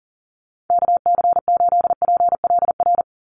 Morse code, heard as a keyed tone, about 40 wpm, 700 Hz. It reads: KC8PLR
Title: USA Call Signs 40wpm